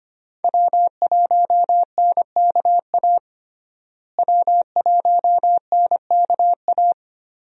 Morse code, heard as a keyed tone, about 25 wpm, 700 Hz. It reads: W1NKA W1NKA
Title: USA Call Signs - 5 Characters - Repeated 3x - 25wpm